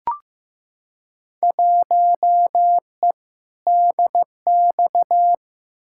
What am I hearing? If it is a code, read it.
Morse code, heard as a keyed tone, about 15 wpm, 700 Hz. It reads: 1E DX
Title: Field Day Exchanges Encore - 15wpm